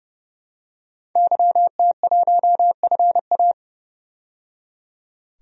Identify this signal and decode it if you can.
Morse code, heard as a keyed tone, about 31 wpm, 700 Hz. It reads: YT1FA
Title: DXCC-100 Call Signs - 5 Characters - 30wpm